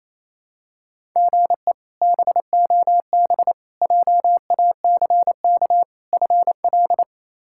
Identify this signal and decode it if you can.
Morse code, heard as a keyed tone, about 28 wpm, 700 Hz. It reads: GE BOB JACK FL
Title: K1USN SST QSO Practice - 28wpm